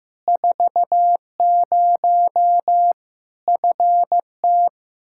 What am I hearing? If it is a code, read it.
Morse code, heard as a keyed tone, about 15 wpm, 700 Hz. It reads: 40 FT
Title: Practice QSO - Exchange 2 - Info - 15wpm